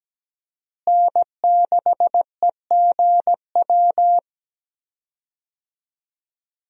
Morse code, heard as a keyed tone, about 17 wpm, 700 Hz. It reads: N6EGW
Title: USA Call Signs - 5 Characters - Repeated 3x - 17wpm